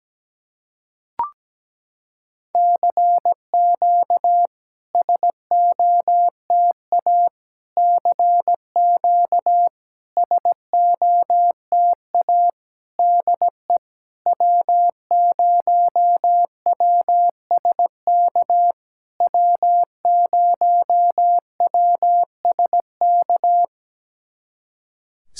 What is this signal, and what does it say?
Morse code, heard as a keyed tone, about 17 wpm, 700 Hz. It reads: CQ SOTA CQ SOTA DE W0WSK W0WSK
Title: SOTA QSO Practice 17wpm